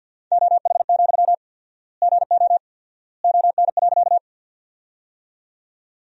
Morse code, heard as a keyed tone, about 53 wpm, 700 Hz. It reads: OH, GO ON.
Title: 4x Word-Spacing - Sentences from Top 500 Words - 50wpm